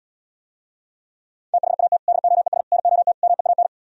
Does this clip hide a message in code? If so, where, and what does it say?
Yes: Morse code from 1.5 to 3.7 s, heard as a keyed tone, about 77 wpm, 700 Hz. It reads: THAT DOES NOT MEAN